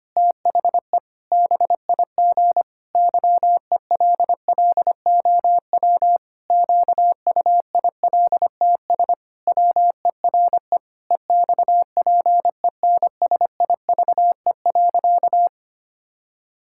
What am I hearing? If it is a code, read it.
Morse code, heard as a keyed tone, about 25 wpm, 700 Hz. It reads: THE BIG YELLOW QUILTS WERE EXPENSIVE.
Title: Pangram Challenge v2 - 25wpm